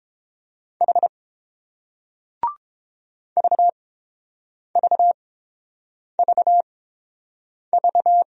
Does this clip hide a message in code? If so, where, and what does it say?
Yes: Morse code from 0.8 to 8.2 s, heard as a keyed tone, about 31 wpm, spaced as for 7 wpm, 700 Hz. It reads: H VVVV